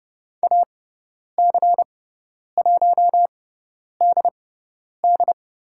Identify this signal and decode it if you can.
Morse code, heard as a keyed tone, about 30 wpm, spaced as for 10 wpm, 700 Hz. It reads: AC1DD
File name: Lesson 16g - Letter c - Call Signs 30wpm